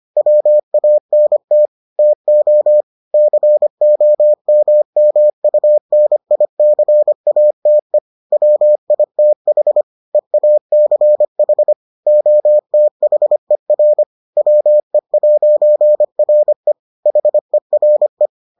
Morse code, heard as a keyed tone, about 25 wpm, 600 Hz. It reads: WANT TO COMMUNICATE WITH EACH OTHER WE'RE HERE